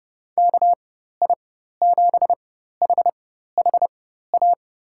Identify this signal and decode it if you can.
Morse code, heard as a keyed tone, about 30 wpm, spaced as for 14 wpm, 700 Hz. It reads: KI7HHA
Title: USA Call Signs - 6 Characters - 4x Character Spacing - 30wpm